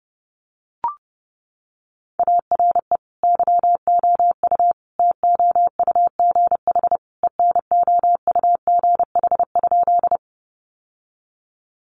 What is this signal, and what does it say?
Morse code, heard as a keyed tone, about 30 wpm, 700 Hz. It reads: ARE YOU TOUGH ENOUGH?
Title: Sentences from Top 1200 Words 30wpm